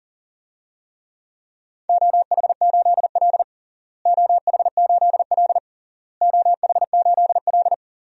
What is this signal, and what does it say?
Morse code, heard as a keyed tone, about 40 wpm, 700 Hz. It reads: OH8L OH8L OH8L
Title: DXCC-100 Call Signs - 4 Characters - Repeated 3x - 40wpm